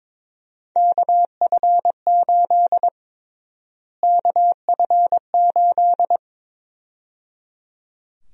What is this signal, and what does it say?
Morse code, heard as a keyed tone, about 22 wpm, 700 Hz. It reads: KF8 KF8
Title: USA Call Sign Prefixes - Repeated 2x - 22wpm